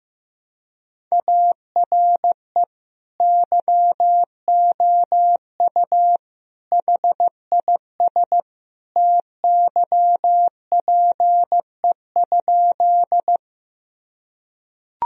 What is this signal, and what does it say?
Morse code, heard as a keyed tone, about 15 wpm, 700 Hz. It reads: ARE YOU HIS TYPE?
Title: Sentences from Top 1000 Words - Review Part 3 - 15wpm